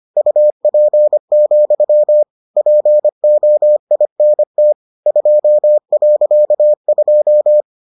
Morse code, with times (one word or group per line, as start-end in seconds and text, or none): 0.2-2.2 UP,
2.6-4.7 POINT
5.1-7.6 2.2